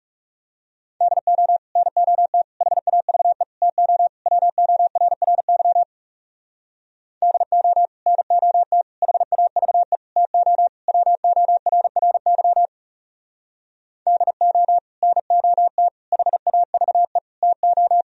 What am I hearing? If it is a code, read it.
Morse code, heard as a keyed tone, about 38 wpm, 700 Hz. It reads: DO NOT HAVE TO WORRY DO NOT HAVE TO WORRY DO NOT HAVE TO